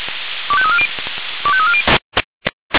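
Five-tone selective call (ZVEI), heard as quick runs of five tones, tones of about 70 ms. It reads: 25340 25340
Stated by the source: ZVEI five-tone selective call received by SDR.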